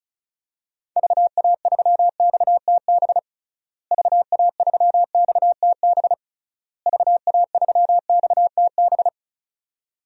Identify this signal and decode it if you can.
Morse code, heard as a keyed tone, about 36 wpm, 700 Hz. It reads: VA3XTB VA3XTB VA3XTB
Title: DXCC-100 Call Signs - 6 Characters - Repeated 3x - 35wpm